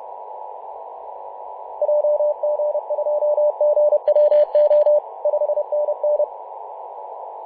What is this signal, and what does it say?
Morse code, heard as a keyed tone, about 31 wpm, 600 Hz, with radio noise. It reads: JG2GWO 5NN
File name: Mar/20 2349z FW5JJ 28.011MHz CW